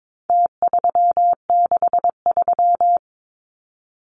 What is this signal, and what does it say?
Morse code, heard as a keyed tone, about 22 wpm, 700 Hz. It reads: T363
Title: Numbers - Contest Style - 4-Digit 22wpm